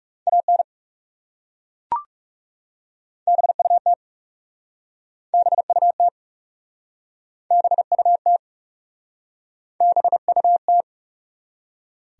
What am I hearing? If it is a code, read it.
Morse code, heard as a keyed tone, about 40 wpm, 700 Hz. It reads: AN BUT BUT BUT BUT